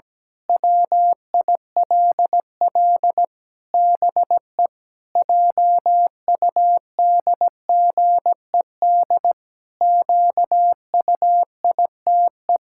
Morse code, heard as a keyed tone, about 17 wpm, 700 Hz. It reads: WILL BE JUDGED QUITE
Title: Pangram Challenge v2 - 17wpm